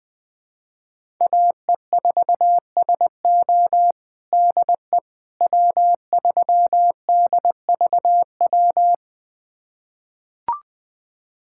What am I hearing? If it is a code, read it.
Morse code, heard as a keyed tone, about 21 wpm, 700 Hz. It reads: AE4SO DE W3DVW
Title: USA Callsign DE Callsign - 5 Characters - 20wpm